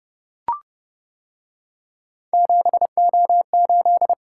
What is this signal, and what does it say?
Morse code, heard as a keyed tone, about 31 wpm, 700 Hz. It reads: 7O8